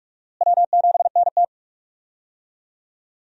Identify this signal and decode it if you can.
Morse code, heard as a keyed tone, about 46 wpm, 700 Hz. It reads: W7NT
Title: USA Call Signs - 4 Characters - Repeated 3x - 45wpm